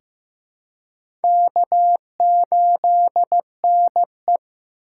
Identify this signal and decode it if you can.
Morse code, heard as a keyed tone, about 15 wpm, 700 Hz. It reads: K8NE
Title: USA Call Signs - 4x Character Spacing - 15wpm